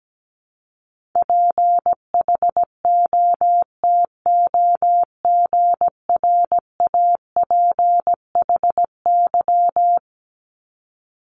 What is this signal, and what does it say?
Morse code, heard as a keyed tone, about 17 wpm, 700 Hz. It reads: PHOTOGRAPHY
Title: Multi-letter Phonemes 17wpm